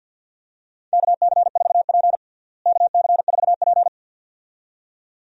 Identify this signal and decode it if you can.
Morse code, heard as a keyed tone, about 53 wpm, 700 Hz. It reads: KC4P KC4P
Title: USA Call Signs - 4 Characters - Repeated 2x - 50wpm